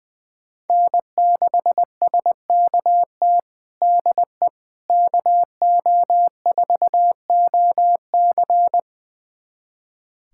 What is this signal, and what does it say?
Morse code, heard as a keyed tone, about 20 wpm, 700 Hz. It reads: N6SKT DE KO4OC
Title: USA Callsign DE Callsign - 5 Characters - 20wpm